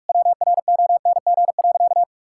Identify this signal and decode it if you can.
Morse code, heard as a keyed tone, about 45 wpm, 690 Hz. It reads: WRONG.
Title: Sentences from Top 1000 Words - Review Part 5 - 45wpm